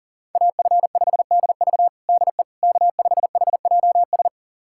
Morse code, heard as a keyed tone, about 40 wpm, 700 Hz. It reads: AF5DV DE K5HJS
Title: USA Callsign DE Callsign - 5 Characters - 40wpm